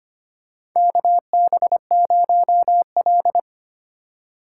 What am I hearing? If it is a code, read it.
Morse code, heard as a keyed tone, about 25 wpm, 700 Hz. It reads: KB0L
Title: USA Call Signs - 4 Characters - 8x Character Spacing - 25wpm